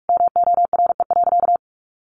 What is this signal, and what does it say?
Morse code, heard as a keyed tone, about 44 wpm, 700 Hz. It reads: MORE.
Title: Sentences from Top 1300 Words 45wpm